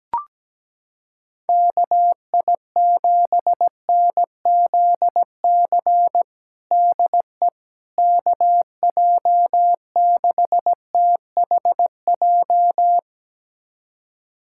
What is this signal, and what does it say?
Morse code, heard as a keyed tone, about 17 wpm, 700 Hz. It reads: KI7NZC DE KJ6THJ
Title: USA Callsign DE Callsign - 6 Characters - 17wpm